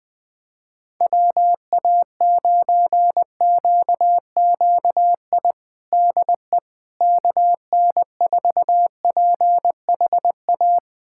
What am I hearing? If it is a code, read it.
Morse code, heard as a keyed tone, about 20 wpm, 700 Hz. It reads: WA9QQI DE KN4PHA
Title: USA Callsign DE Callsign 20wpm